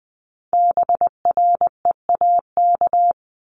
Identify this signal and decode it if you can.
Morse code, heard as a keyed tone, about 19 wpm, 700 Hz. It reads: BREAK
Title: Multi-letter Phonemes 20wpm